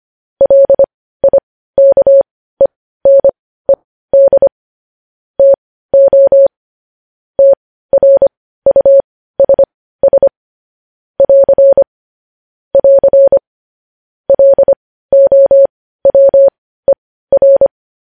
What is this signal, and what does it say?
Morse code, heard as a keyed tone, about 25 wpm, spaced as for 15 wpm, 550 Hz. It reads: LIKENED TO TRUSS <AR> <AR> LOWER
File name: podcast / News headlines-1x-25x15